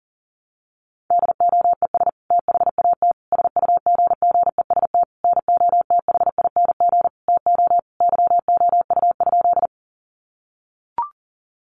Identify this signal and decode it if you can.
Morse code, heard as a keyed tone, about 40 wpm, 700 Hz. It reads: DOES THAT SUGGEST NOTHING TO YOU?
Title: Sentences from Top 1300 Words 40wpm